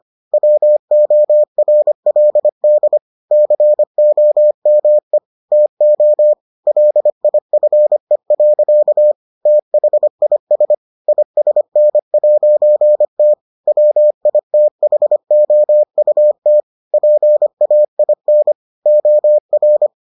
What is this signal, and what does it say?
Morse code, heard as a keyed tone, about 25 wpm, 600 Hz. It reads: WORLD COME TO LIFE. THIS ISN'T WITHOUT PAIN OR